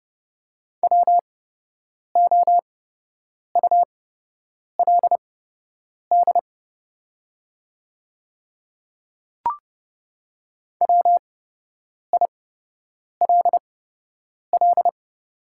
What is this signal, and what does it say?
Morse code, heard as a keyed tone, about 30 wpm, spaced as for 8 wpm, 700 Hz. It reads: WOULD WILL
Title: 8x Character Spacing - Top 300 Words - 30wpm